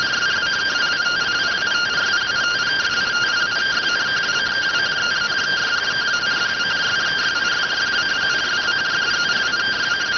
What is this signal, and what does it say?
Two-tone teleprinter signal, about 170 Hz shift.